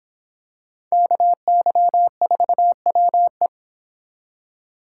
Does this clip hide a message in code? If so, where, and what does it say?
Yes: Morse code from 0.9 to 3.5 s, heard as a keyed tone, about 25 wpm, 700 Hz. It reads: KY4WE